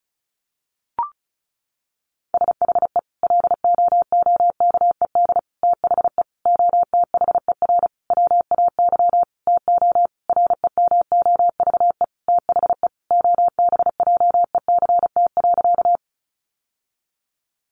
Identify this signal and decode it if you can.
Morse code, heard as a keyed tone, about 34 wpm, 700 Hz. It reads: SHE LOOKED THE OTHER WAY TO REMOVE THE OBJECT.
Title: Sentences from Top 800 Words 35wpm